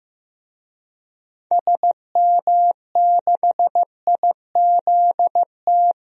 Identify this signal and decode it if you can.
Morse code, heard as a keyed tone, about 15 wpm, 700 Hz. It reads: SM6IZT
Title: DXCC-100 Call Signs - 6 Characters - Repeated 3x - 15wpm